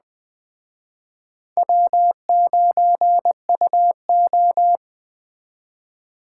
Morse code, heard as a keyed tone, about 20 wpm, 700 Hz. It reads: W9UO
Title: Speed Racing - USA Call Signs 17wpm